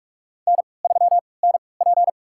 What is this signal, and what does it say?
Morse code, heard as a keyed tone, about 47 wpm, spaced as for 26 wpm, 700 Hz. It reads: N3NP
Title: USA Call Signs - 4 Characters - 3x Character Spacing - 45wpm